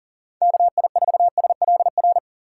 Morse code, heard as a keyed tone, about 40 wpm, 700 Hz. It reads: KI4SLR
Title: USA Call Signs - 6 Characters - 2x Character Spacing - 40wpm